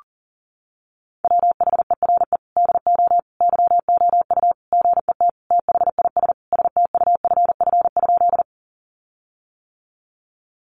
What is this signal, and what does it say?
Morse code, heard as a keyed tone, about 37 wpm, 700 Hz. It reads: WHERE DO YOU GET THIS STUFF?
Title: Sentences from Top 1400 Words 40wpm